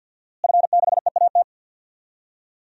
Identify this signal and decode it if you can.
Morse code, heard as a keyed tone, about 53 wpm, 700 Hz. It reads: F6EAT
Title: DXCC-100 Call Signs - 5 Characters - Repeated 3x - 50wpm